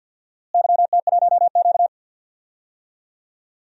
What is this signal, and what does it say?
Morse code, heard as a keyed tone, about 47 wpm, 700 Hz. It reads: YT1Q
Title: DXCC-100 Call Signs - 4 Characters - Repeated 3x - 50wpm